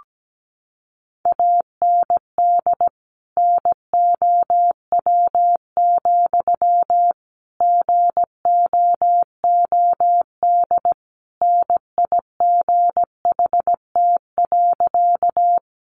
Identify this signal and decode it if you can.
Morse code, heard as a keyed tone, about 17 wpm, 700 Hz. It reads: AND NOW, GOOD NIGHT.
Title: Sentences from Top 300 Words 17wpm